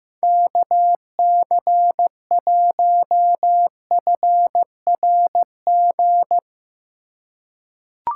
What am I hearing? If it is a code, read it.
Morse code, heard as a keyed tone, about 15 wpm, 700 Hz. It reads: KC1FRG
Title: USA Call Signs - 6 Characters - 2x Character Spacing - 15wpm